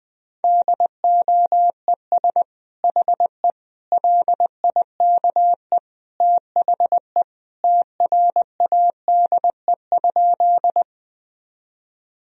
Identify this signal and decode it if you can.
Morse code, heard as a keyed tone, about 20 wpm, 700 Hz. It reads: DOES HE LIKE THE TRADE?
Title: Sentences from Top 1000 Words - Review Part 4 - 20wpm